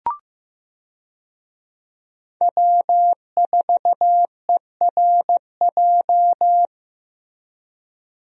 Morse code, heard as a keyed tone, about 15 wpm, 700 Hz. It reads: W4ERJ
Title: USA Call Signs - 5 Characters - Repeated 3x - 15wpm